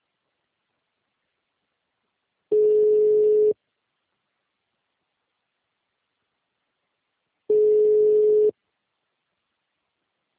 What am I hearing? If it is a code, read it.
Ringback tone (Europe, 425 Hz).
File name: out- -2004- - - .96840